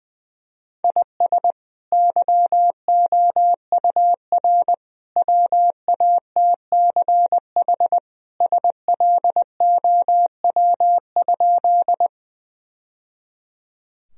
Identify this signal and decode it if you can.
Morse code, heard as a keyed tone, about 21 wpm, 700 Hz. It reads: IS YOUR WATCH SLOW?
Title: Sentences from Top 1000 Words - Review Part 2 - 20wpm